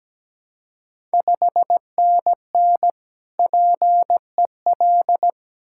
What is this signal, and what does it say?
Morse code, heard as a keyed tone, about 17 wpm, 690 Hz. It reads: 5NN PEL
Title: Ontario QSO Party Exchange 17wpm